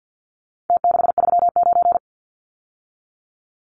Morse code, heard as a keyed tone, about 52 wpm, 700 Hz. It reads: T639
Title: Numbers - Contest Style - 4-Digit 50wpm